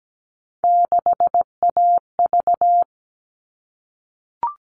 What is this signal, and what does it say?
Morse code, heard as a keyed tone, about 17 wpm, 700 Hz. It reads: 6AV
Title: All Cut Numbers - 3-Digit 17wpm